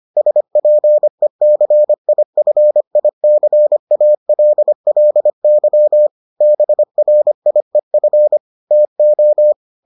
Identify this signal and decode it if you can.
Morse code, heard as a keyed tone, about 25 wpm, 600 Hz. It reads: SPECIFICALLY BRIEF TO